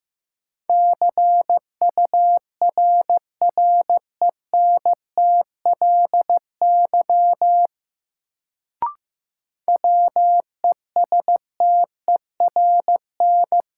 Morse code, heard as a keyed tone, about 15 wpm, 700 Hz. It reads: CURRENTLY WESTERN
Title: Sentences from Top 1000 Words - Review Part 7 - New Word Drill - 15wpm